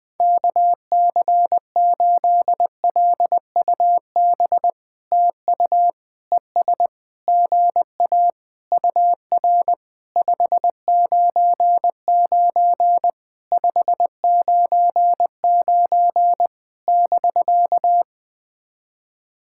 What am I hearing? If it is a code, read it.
Morse code, heard as a keyed tone, about 20 wpm, 700 Hz. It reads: KC8LUB TU ES GA UR 599 599 <BK>
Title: POTA QSO Practice - 20wpm